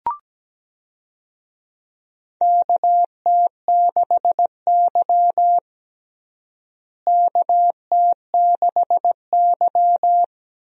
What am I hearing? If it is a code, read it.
Morse code, heard as a keyed tone, about 17 wpm, 700 Hz. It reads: KT6Y KT6Y
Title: USA Call Signs - 4 Characters - Repeated 3x - 17wpm